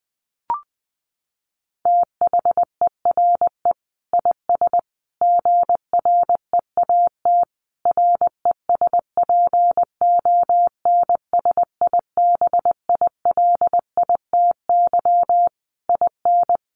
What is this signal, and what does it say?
Morse code, heard as a keyed tone, about 20 wpm, 700 Hz. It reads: THERE IS GREAT RESPONSIBILITY IN
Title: Sentences from Top 1200 Words 20wpm